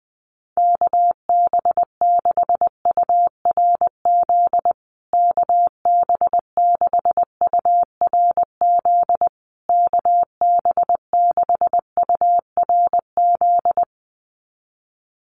Morse code, heard as a keyed tone, about 20 wpm, 700 Hz. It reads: KB6URZ KB6URZ KB6URZ
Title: USA Call Signs Repeated 3x 20wpm